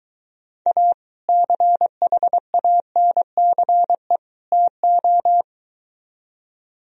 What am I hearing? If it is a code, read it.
Morse code, heard as a keyed tone, about 22 wpm, 700 Hz. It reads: A CHANCE TO